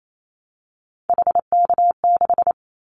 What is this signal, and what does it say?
Morse code, heard as a keyed tone, about 28 wpm, 700 Hz. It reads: HK6
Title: Callsign Prefixes for Top 100 DXCC - 28wpm